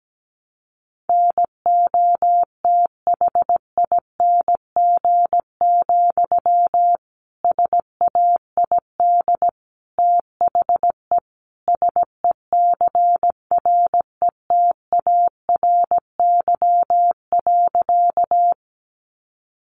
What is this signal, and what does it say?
Morse code, heard as a keyed tone, about 17 wpm, 700 Hz. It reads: NOTHING, SAID THE SECRETARY.
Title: Sentences from Top 1300 Words 17wpm